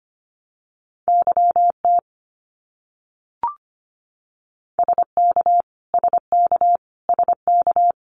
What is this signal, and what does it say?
Morse code, heard as a keyed tone, about 25 wpm, 700 Hz. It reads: YT SK SK SK
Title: Canadian Provinces and Territories - Repeated 3x - 25wpm